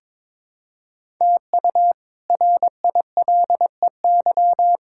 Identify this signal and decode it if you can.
Morse code, heard as a keyed tone, about 22 wpm, 700 Hz. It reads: TU RILEY
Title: K1USN SST QSO Practice - 22wpm